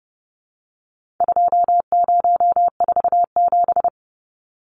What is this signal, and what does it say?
Morse code, heard as a keyed tone, about 30 wpm, 700 Hz. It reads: 2047
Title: Numbers - 4-Digit 30wpm